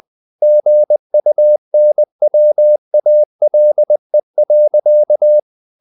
Morse code, heard as a keyed tone, about 20 wpm, 600 Hz. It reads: GUNWALE.
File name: the old man and the sea part 2 20wpm